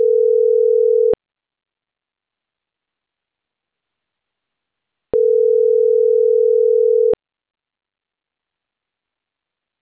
Ringback tone (North America), 440 + 480 Hz.